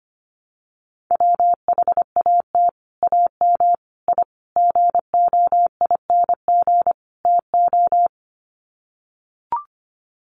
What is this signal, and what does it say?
Morse code, heard as a keyed tone, about 25 wpm, 690 Hz. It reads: WHAT AM I GOING TO
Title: Sets of 5 Words Encore 25wpm